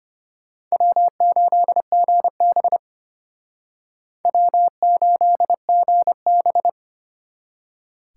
Morse code, heard as a keyed tone, about 32 wpm, 700 Hz. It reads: W8GB W8GB